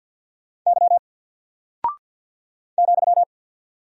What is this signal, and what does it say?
Morse code, heard as a keyed tone, about 48 wpm, 700 Hz. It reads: Y ,
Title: Single Character (Letter:Number:Punctuation:Prosign) Alt 50wpm